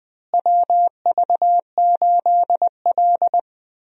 Morse code, heard as a keyed tone, about 20 wpm, 700 Hz. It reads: WV8L
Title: USA Call Signs - 4 Characters - Repeated 2x - 20wpm